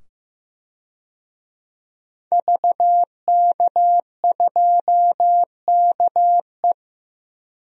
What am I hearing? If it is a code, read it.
Morse code, heard as a keyed tone, about 15 wpm, 700 Hz. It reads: VK2KE
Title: DXCC-100 Call Signs - 5 Characters - 15wpm